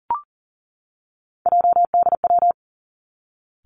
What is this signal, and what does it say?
Morse code, heard as a keyed tone, about 41 wpm, 700 Hz. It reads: JDW
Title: Three Challenging Letter-Number Combinations 40wpm